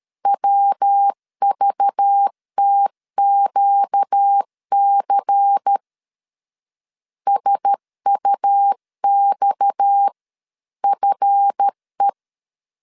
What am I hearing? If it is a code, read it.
Morse code, heard as a keyed tone, about 13 wpm, 800 Hz. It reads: WVTQC SUXFE
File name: LEZIONI / 7 GF9Z / 7 GF 12wpm 2122